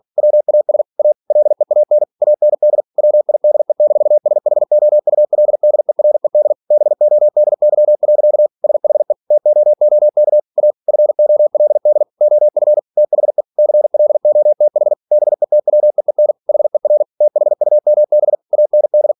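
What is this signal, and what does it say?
Morse code, heard as a keyed tone, about 46 wpm, 600 Hz. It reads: WAS A LEAN AND WIDE-SHOULDERED BODY. SHE TOOK A FOLD OF THE CLOTH BETWEEN HER THUMB AND
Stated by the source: Practice text from an English novel sent at 47 WPM.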